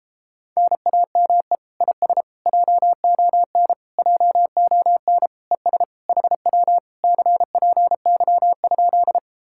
Morse code, heard as a keyed tone, about 33 wpm, 700 Hz. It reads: NAME IS JON JON ES HW CPY?